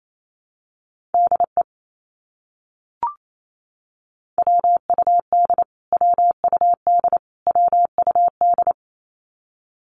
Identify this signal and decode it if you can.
Morse code, heard as a keyed tone, about 28 wpm, 700 Hz. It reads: DE WUD WUD WUD
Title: Abbreviations Repeated 3x - 28wpm